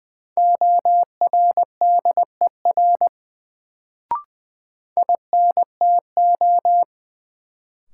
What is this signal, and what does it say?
Morse code, heard as a keyed tone, about 20 wpm, 700 Hz. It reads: ORDER INTO
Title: Sets of 2 Words - New Word Drill - 20wpm